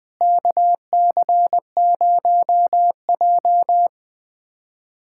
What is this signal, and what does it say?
Morse code, heard as a keyed tone, about 20 wpm, 700 Hz. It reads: KC0J
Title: USA Callsign DE Callsign - 4 Characters - 20wpm